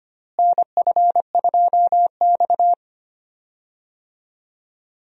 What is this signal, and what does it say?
Morse code, heard as a keyed tone, about 25 wpm, 700 Hz. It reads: NF2X
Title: USA Call Signs - 4 Characters - Repeated 2x - 25wpm